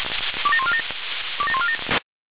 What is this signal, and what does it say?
Five-tone selective call (ZVEI), heard as quick runs of five tones, tones of about 70 ms. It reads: 27147 27147